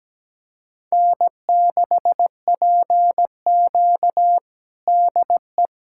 Morse code, heard as a keyed tone, about 17 wpm, 700 Hz. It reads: N6PQ DE
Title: USA Callsign DE Callsign - 4 Characters - 17wpm